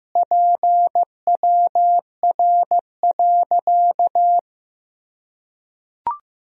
Morse code, heard as a keyed tone, about 15 wpm, 700 Hz. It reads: PWR.
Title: Sentences from Top 500 Words and Abbreviations 15wpm